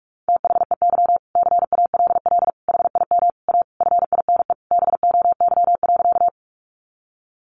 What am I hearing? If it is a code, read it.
Morse code, heard as a keyed tone, about 44 wpm, 700 Hz. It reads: THEY CALL HIM A FINE BOY.
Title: Sentences from Top 500 Words 45wpm